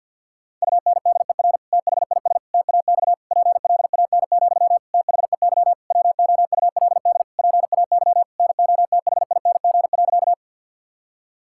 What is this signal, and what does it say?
Morse code, heard as a keyed tone, about 49 wpm, 690 Hz. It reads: UNDER THIS TAX PLAN, THEY WOULD PAY NOTHING.
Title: Sentences from Top 1000 Words - Review Part 5 - 50wpm